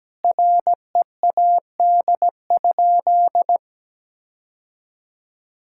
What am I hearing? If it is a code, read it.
Morse code, heard as a keyed tone, about 17 wpm, 700 Hz. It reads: READ?
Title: Sentences from Top 1000 Words - Review Part 5 - 17wpm